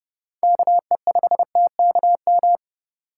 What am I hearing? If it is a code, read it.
Morse code, heard as a keyed tone, about 30 wpm, 700 Hz. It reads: KE5TKM
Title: USA Call Signs - 6 Characters - 3x Character Spacing - 30wpm